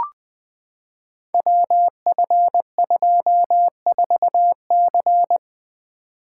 Morse code, heard as a keyed tone, about 20 wpm, 700 Hz. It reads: WF24C
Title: Code Groups - 5 Letter:Numbers - 20wpm